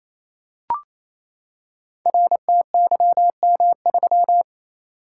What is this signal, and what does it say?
Morse code, heard as a keyed tone, about 29 wpm, 700 Hz. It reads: RTYM3
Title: Code Groups - 5 Letter:Numbers - 28wpm